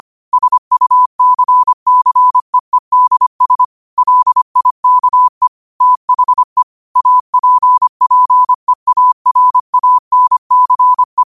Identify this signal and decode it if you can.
Morse code, heard as a keyed tone, about 25 wpm, 1000 Hz. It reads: SUCCEEDS LIKE THE APPEARANCE
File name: qotdfiles / 25wpm